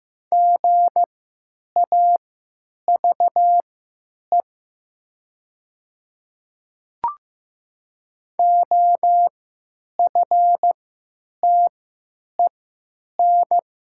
Morse code, heard as a keyed tone, about 15 wpm, spaced as for 9 wpm, 700 Hz. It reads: GAVE OFTEN
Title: 3x Character Spacing - Top 500 Words - 15wpm